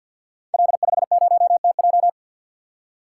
Morse code, heard as a keyed tone, about 52 wpm, 700 Hz. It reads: L50TJ
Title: Code Groups - 5 Letter:Numbers - 50wpm